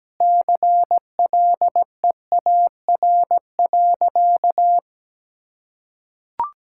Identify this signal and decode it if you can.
Morse code, heard as a keyed tone, about 17 wpm, 700 Hz. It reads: CLEAR.
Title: Sentences from Top 1000 Words - Review Part 6 - 17wpm